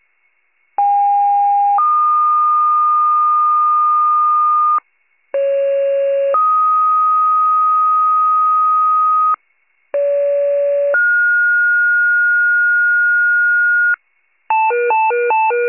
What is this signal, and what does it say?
Two-tone page, heard as a one-second tone, then a three-second tone, about 810 then 1220 Hz, about 550 then 1250 Hz, about 570 then 1430 Hz. It is heard from 0.8 to 14.0 s.
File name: pager / 94 RESCUE / wcalert